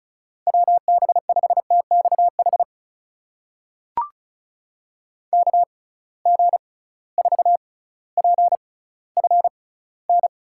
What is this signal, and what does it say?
Morse code, heard as a keyed tone, about 34 wpm, spaced as for 18 wpm, 700 Hz. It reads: WB5TXH KG4PFN
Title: USA Call Signs - 6 Characters - 6x Character Spacing - 35wpm